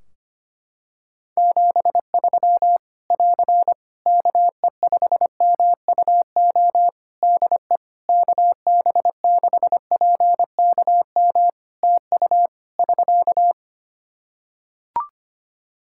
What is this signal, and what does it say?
Morse code, heard as a keyed tone, about 25 wpm, 700 Hz. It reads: 73 <AR> KE5MUO DE KB6PKM TU <SK>
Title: Practice QSO - Exchange 4 - Ending - 25wpm